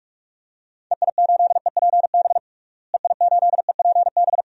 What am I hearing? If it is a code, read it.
Morse code, heard as a keyed tone, about 46 wpm, 700 Hz. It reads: EI8EPB EI8EPB
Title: DXCC-100 Call Signs - 6 Characters - Repeated 2x - 45wpm